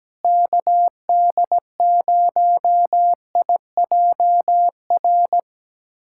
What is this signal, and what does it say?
Morse code, heard as a keyed tone, about 17 wpm, 700 Hz. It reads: KD0IJR
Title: USA Callsign DE Callsign - 6 Characters - 17wpm